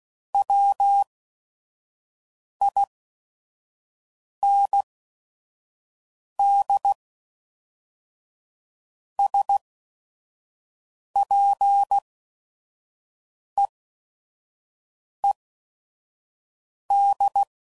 Morse code, heard as a keyed tone, about 16 wpm, spaced as for 5 wpm, 800 Hz. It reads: WIND SPEED
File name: CW Code Groups @5WPM / Weather Sets / Weather 5 12